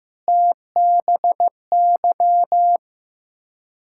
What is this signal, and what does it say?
Morse code, heard as a keyed tone, about 15 wpm, 690 Hz. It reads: TBY
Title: Ontario QSO Party Exchange 15wpm